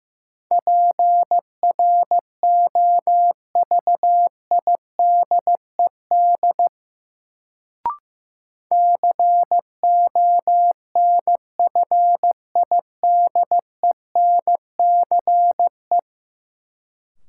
Morse code, heard as a keyed tone, about 15 wpm, 700 Hz. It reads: PROVIDED CONFIDENCE
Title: Sentences from Top 1500 Words - New Word Drill Alt - 15wpm